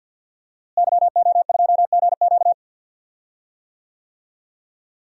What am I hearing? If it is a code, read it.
Morse code, heard as a keyed tone, about 52 wpm, 700 Hz. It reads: YO2GQ
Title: DXCC-100 Call Signs - 5 Characters - Repeated 2x - 50wpm